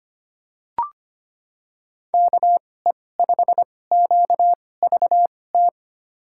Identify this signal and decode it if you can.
Morse code, heard as a keyed tone, about 25 wpm, spaced as for 18 wpm, 700 Hz. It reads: KE5QVT
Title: USA Call Signs - 6 Characters - 2x Character Spacing - 25wpm